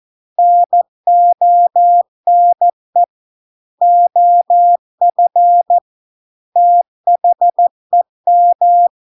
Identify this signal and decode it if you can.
Morse code, heard as a keyed tone, about 14 wpm, 700 Hz. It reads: NONE OF THEM